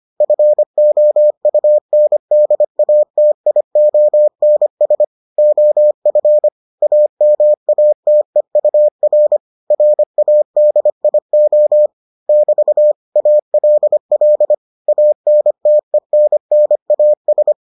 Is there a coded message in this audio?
Yes: Morse code from 0.2 to 17.5 s, heard as a keyed tone, about 25 wpm, 600 Hz. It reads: FOUNDATIONS OF AMATEUR RADIO <BT> ALL ANTENNAS